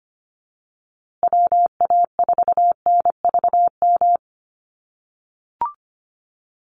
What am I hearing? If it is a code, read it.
Morse code, heard as a keyed tone, about 25 wpm, 700 Hz. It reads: WA4NVM
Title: USA Call Signs Repeated 2x 25wpm